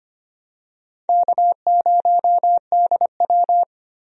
Morse code, heard as a keyed tone, about 25 wpm, 700 Hz. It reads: K0DW
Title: USA Call Signs - 4 Characters - Repeated 3x - 25wpm